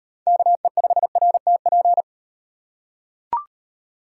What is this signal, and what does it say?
Morse code, heard as a keyed tone, about 37 wpm, 700 Hz. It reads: KE5RTP